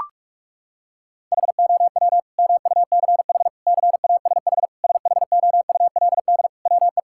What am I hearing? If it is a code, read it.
Morse code, heard as a keyed tone, about 41 wpm, 700 Hz. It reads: HOW MUCH CASH SHOULD WE
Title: Sentences from Top 1000 Words - Review Part 5 - 45wpm